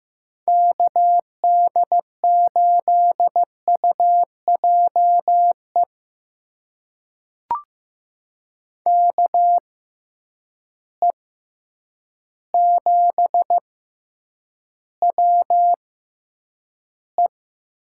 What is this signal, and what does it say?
Morse code, heard as a keyed tone, about 15 wpm, spaced as for 8 wpm, 700 Hz. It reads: KD8UJE KE7WE
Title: USA Call Signs - 6 Characters - 6x Character Spacing - 15wpm